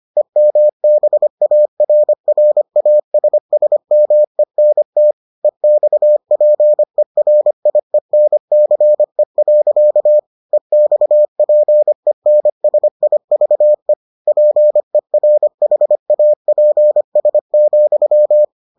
Morse code, heard as a keyed tone, about 25 wpm, 600 Hz. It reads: EMBARRASSMENT EXPERIENCE. EXPENSIVE PERHAPS,